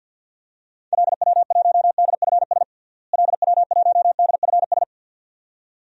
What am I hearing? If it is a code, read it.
Morse code, heard as a keyed tone, about 52 wpm, 700 Hz. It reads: LW1DFS LW1DFS
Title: DXCC-100 Call Signs - 6 Characters - Repeated 2x - 50wpm